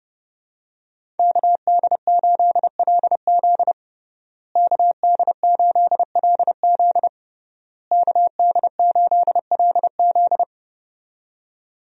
Morse code, heard as a keyed tone, about 32 wpm, 700 Hz. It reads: KD8LZ KD8LZ KD8LZ
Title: USA Call Signs - 5 Characters - Repeated 3x - 30wpm